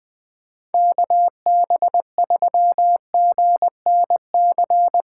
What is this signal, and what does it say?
Morse code, heard as a keyed tone, about 21 wpm, 700 Hz. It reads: KB3GNC
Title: USA Callsign DE Callsign - 6 Characters - 20wpm